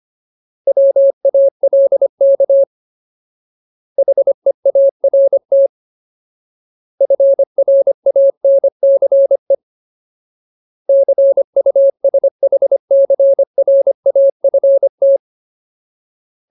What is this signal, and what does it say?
Morse code, heard as a keyed tone, about 25 wpm, 550 Hz. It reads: WALK HEART FRANCE CUSHCRAFT
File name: Ultimate Word List / 25WPM Ultimate